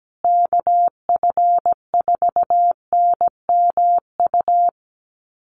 Morse code, heard as a keyed tone, about 17 wpm, 700 Hz. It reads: KF4NMU
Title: USA Call Signs Repeated 3x 17wpm